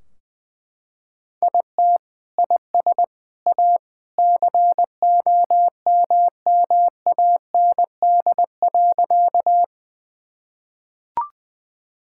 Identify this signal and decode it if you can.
Morse code, heard as a keyed tone, about 20 wpm, 700 Hz. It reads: IT IS A COMMAND.
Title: Sentences from Top 1600 Words 20wpm